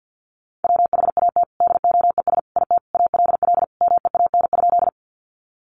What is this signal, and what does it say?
Morse code, heard as a keyed tone, about 50 wpm, 700 Hz. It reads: WHAT DOES IT ALL MEAN?
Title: Sentences from Top 200 Words 50wpm